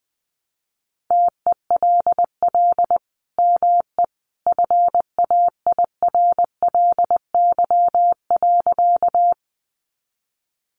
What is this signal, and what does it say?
Morse code, heard as a keyed tone, about 20 wpm, 700 Hz. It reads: TELL ME FAIRLY.
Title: Sentences from Top 1100 Words 20wpm